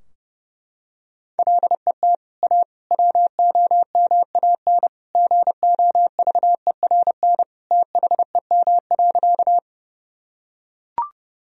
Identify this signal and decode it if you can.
Morse code, heard as a keyed tone, about 30 wpm, 700 Hz. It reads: LET A WOMAN GOVERN THEM.
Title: Sentences from Top 1000 Words - Review Part 2 - 30wpm